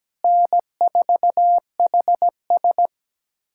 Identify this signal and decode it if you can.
Morse code, heard as a keyed tone, about 17 wpm, 700 Hz. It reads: N4HS
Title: USA Callsign DE Callsign - 4 Characters - 17wpm